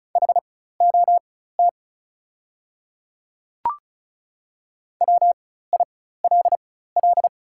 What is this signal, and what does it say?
Morse code, heard as a keyed tone, about 36 wpm, spaced as for 16 wpm, 690 Hz. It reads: HOT WILL
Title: 4x Character Spacing - Top 100 Words - 35wpm